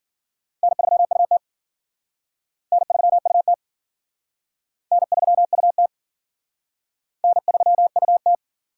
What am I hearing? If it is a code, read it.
Morse code, heard as a keyed tone, about 50 wpm, 700 Hz. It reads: N3UT N3UT N3UT N3UT